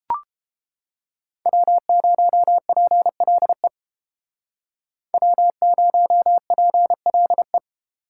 Morse code, heard as a keyed tone, about 30 wpm, 700 Hz. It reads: W0PLE W0PLE